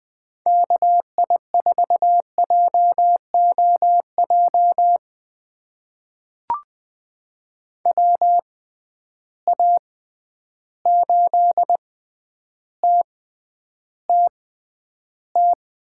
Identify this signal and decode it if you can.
Morse code, heard as a keyed tone, about 20 wpm, spaced as for 10 wpm, 700 Hz. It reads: KI4JOJ WA8TTT
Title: USA Call Signs - 6 Characters - 6x Character Spacing - 20wpm